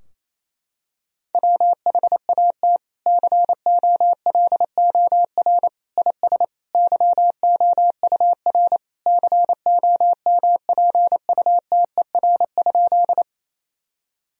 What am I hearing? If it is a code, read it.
Morse code, heard as a keyed tone, about 28 wpm, 700 Hz. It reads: WHAT COLOR IS YOUR COMPUTER?
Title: Sentences from Top 1000 Words - Review Part 3 - 28wpm